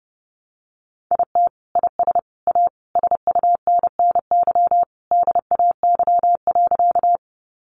Morse code, heard as a keyed tone, about 30 wpm, 700 Hz. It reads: IT IS A SUNNY DAY.
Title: Easy Phrases 30wpm